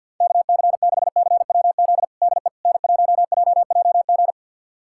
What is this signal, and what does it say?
Morse code, heard as a keyed tone, about 51 wpm, 700 Hz. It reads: KC6CWZ DE N1JJG
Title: USA Callsign DE Callsign 50wpm